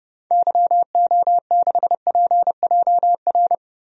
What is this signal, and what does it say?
Morse code, heard as a keyed tone, about 30 wpm, 700 Hz. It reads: YO6PJR
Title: DXCC-100 Call Signs - 6 Characters - 30wpm